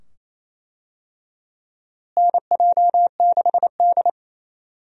Morse code, heard as a keyed tone, about 28 wpm, 700 Hz. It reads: NJ6D
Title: USA Call Signs - 4 Characters - Repeated 2x - 28wpm